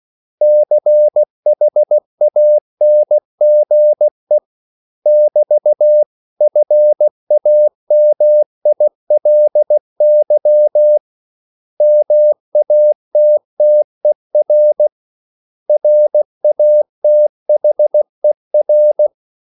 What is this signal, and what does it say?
Morse code, heard as a keyed tone, about 16 wpm, 600 Hz. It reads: CHANGE <BT> FAMILY MATTER RATHER
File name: CW / Practice / 500MostCommonEnglishWords16